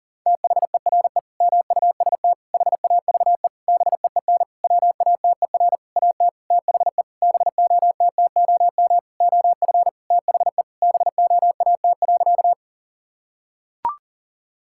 Morse code, heard as a keyed tone, about 40 wpm, 700 Hz. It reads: THERE MUST HAVE BEEN WATER AT THE BOTTOM OF THE BOAT.
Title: Sentences from Top 1000 Words - Review Part 6 - 40wpm